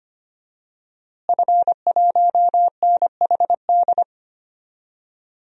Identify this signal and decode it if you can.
Morse code, heard as a keyed tone, about 25 wpm, 700 Hz. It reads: F1NHD
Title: DXCC-100 Call Signs - 5 Characters - Repeated 2x - 25wpm